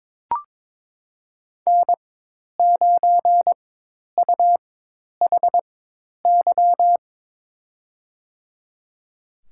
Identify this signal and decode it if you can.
Morse code, heard as a keyed tone, about 22 wpm, spaced as for 10 wpm, 700 Hz. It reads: N9UHY
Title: USA Call Signs - 5 Characters - 4x Character Spacing - 22wpm